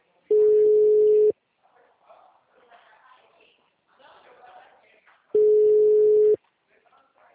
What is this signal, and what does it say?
Ringback tone (Europe, 425 Hz), with speech.